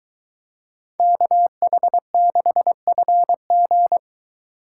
Morse code, heard as a keyed tone, about 22 wpm, 700 Hz. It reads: KH6FG